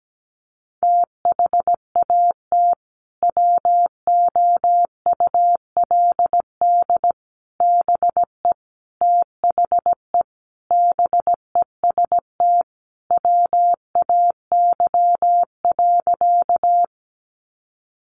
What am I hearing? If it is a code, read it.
Morse code, heard as a keyed tone, about 17 wpm, 700 Hz. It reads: THAT WOULD BE THE BEST WAY.
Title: Sentences from Top 400 Words 17wpm